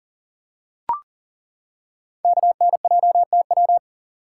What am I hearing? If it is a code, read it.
Morse code, heard as a keyed tone, about 42 wpm, 700 Hz. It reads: KNJTW
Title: Code Groups - 5 Letter:Numbers - 40wpm